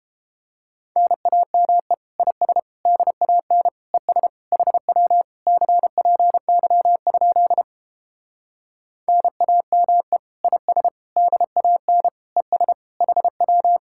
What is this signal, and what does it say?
Morse code, heard as a keyed tone, about 29 wpm, 700 Hz. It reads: NAME IS DAN ES HW CPY? NAME IS DAN ES HW